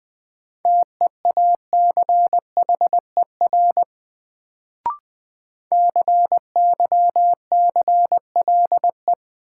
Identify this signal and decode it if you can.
Morse code, heard as a keyed tone, about 21 wpm, 700 Hz. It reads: TEACHER CYCLE
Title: Sentences from Top 900 Words - New Word Drill - 20wpm